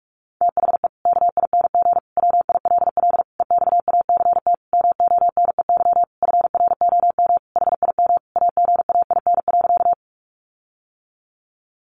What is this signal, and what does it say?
Morse code, heard as a keyed tone, about 44 wpm, 700 Hz. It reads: THE KING WILL EXACT MONEY FROM HIM AGAIN.
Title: Sentences from Top 1300 Words 45wpm